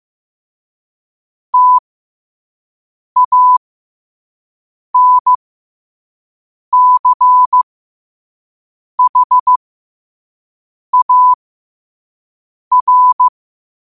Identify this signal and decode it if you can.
Morse code, heard as a keyed tone, about 15 wpm, spaced as for 5 wpm, 1000 Hz. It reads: TANCHAR
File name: qotdfiles / 5wpm